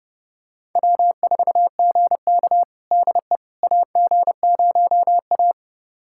Morse code, heard as a keyed tone, about 32 wpm, 700 Hz. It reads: W4GK DE AG0A
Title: USA Callsign DE Callsign - 4 Characters - 30wpm